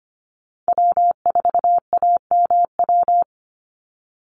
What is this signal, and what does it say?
Morse code, heard as a keyed tone, about 25 wpm, 700 Hz. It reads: W4AMW
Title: NA SOTA Chaser Callsigns 25wpm